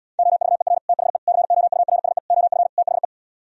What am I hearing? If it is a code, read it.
Morse code, heard as a keyed tone, about 77 wpm, 700 Hz. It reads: OVER THE COURSE OF THE